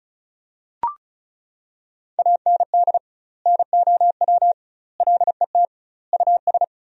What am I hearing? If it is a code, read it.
Morse code, heard as a keyed tone, about 34 wpm, 700 Hz. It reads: AND NOW LET US
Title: 2x Word-Spacing - Sentences from Top 300 Words - 35wpm